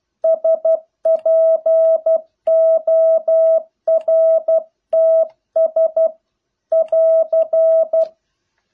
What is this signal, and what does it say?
Morse code, heard as a keyed tone, about 12 wpm, 650 Hz. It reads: SPORTS <AR>